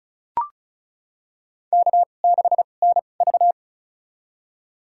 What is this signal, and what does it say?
Morse code, heard as a keyed tone, about 36 wpm, spaced as for 25 wpm, 700 Hz. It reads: K6NV
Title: USA Call Signs - 4 Characters - 2x Character Spacing - 35wpm